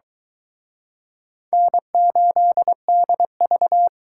Morse code, heard as a keyed tone, about 23 wpm, 700 Hz. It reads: N8DV